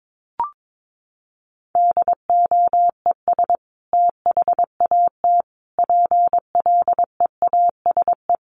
Morse code, heard as a keyed tone, about 22 wpm, 700 Hz. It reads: DOES THAT PLEASE
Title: Sentences from Top 1500 Words 22wpm